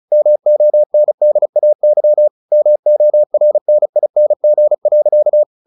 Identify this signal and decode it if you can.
Morse code, heard as a keyed tone, about 35 wpm, 600 Hz. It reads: MONDAY MORNING.
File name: the old man and the sea 2 35wpm